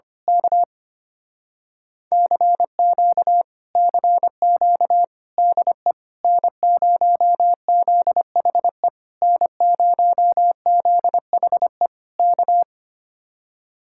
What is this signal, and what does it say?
Morse code, heard as a keyed tone, about 24 wpm, 700 Hz. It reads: K CQ CQ DE N0ZHE N0ZHE K
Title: Speed-Racing Common QSO Phrases - 25wpm